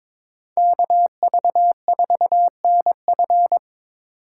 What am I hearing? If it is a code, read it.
Morse code, heard as a keyed tone, about 22 wpm, 700 Hz. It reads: KV4NF
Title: USA Call Signs - 3x Character Spacing - 22wpm